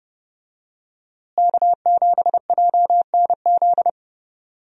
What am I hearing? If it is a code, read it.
Morse code, heard as a keyed tone, about 32 wpm, 700 Hz. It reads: K7JNZ
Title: USA Call Signs - 5 Characters - Repeated 3x - 30wpm